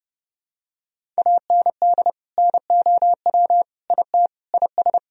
Morse code, heard as a keyed tone, about 30 wpm, 700 Hz. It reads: AND NOW IT IS
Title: Sentences from Top 1000 Words - Review Part 1 - 30wpm